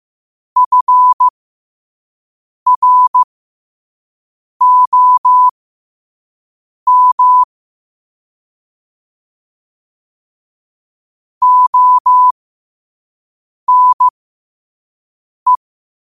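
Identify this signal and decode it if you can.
Morse code, heard as a keyed tone, about 15 wpm, spaced as for 5 wpm, 1000 Hz. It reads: FROM ONE
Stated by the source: Morse code practice at 5 wpm.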